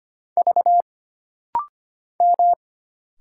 Morse code, heard as a keyed tone, about 25 wpm, 700 Hz. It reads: V M
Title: Single Letters - Rapid-Fire 25wpm